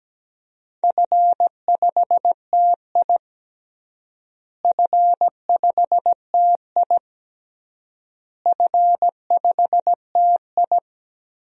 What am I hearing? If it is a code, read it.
Morse code, heard as a keyed tone, about 17 wpm, 700 Hz. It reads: F5TI F5TI F5TI
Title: DXCC-100 Call Signs - 4 Characters - Repeated 3x - 17wpm